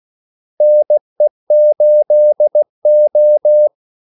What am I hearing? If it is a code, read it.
Morse code, heard as a keyed tone, about 16 wpm, 600 Hz. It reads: NE8O
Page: Kaldesignaler 16 wpm